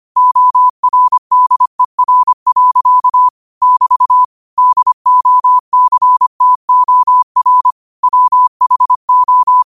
Morse code, heard as a keyed tone, about 25 wpm, 1000 Hz. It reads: ORDER. <BT> DOCTOR WHO